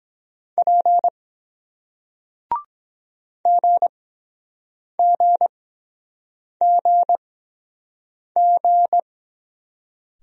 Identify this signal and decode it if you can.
Morse code, heard as a keyed tone, about 23 wpm, spaced as for 7 wpm, 700 Hz. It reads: P GGGG